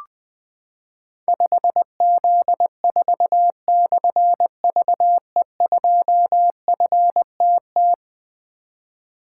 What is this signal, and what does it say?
Morse code, heard as a keyed tone, about 20 wpm, 700 Hz. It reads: 5Z4/VE2FTT
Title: Difficult Call Signs 20wpm